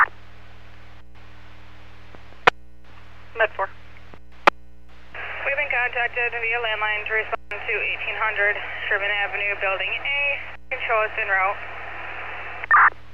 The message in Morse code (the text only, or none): none